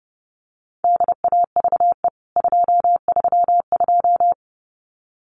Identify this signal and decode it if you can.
Morse code, heard as a keyed tone, about 30 wpm, 700 Hz. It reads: DAVE 232
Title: CWT Exchange 30wpm